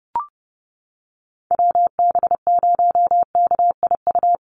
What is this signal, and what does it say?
Morse code, heard as a keyed tone, about 30 wpm, 700 Hz. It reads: WB0KIU
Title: NA SOTA Chaser Callsigns 30wpm